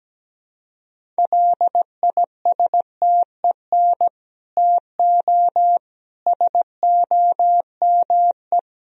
Morse code, heard as a keyed tone, about 17 wpm, 700 Hz. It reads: LISTEN TO SOME
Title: Sentences from Top 1000 Words - Review Part 1 - 17wpm